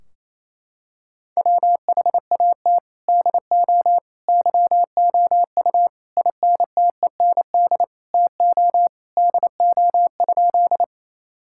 Morse code, heard as a keyed tone, about 28 wpm, 700 Hz. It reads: WHAT DO YOU INTEND TO DO?
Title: Sentences from Top 1600 Words 28wpm